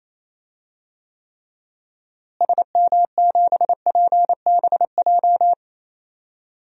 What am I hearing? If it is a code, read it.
Morse code, heard as a keyed tone, about 28 wpm, 700 Hz. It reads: SM7PBJ
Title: DXCC-100 Call Signs - 6 Characters - Repeated 2x - 28wpm